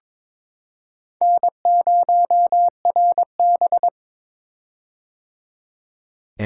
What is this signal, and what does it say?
Morse code, heard as a keyed tone, about 22 wpm, 700 Hz. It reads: N0RB
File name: practice / 22wpm / Four Letter-Number Combinations - 22wpm